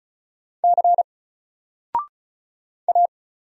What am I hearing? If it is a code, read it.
Morse code, heard as a keyed tone, about 36 wpm, 700 Hz. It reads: C A
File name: Single-Letter-Number-35wpm